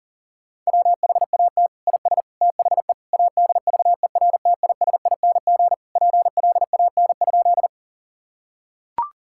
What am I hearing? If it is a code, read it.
Morse code, heard as a keyed tone, about 40 wpm, 700 Hz. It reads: WHAT IS THE ADVERTISING PLAN?
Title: Sentences from Top 1000 Words - Review Part 6 - 40wpm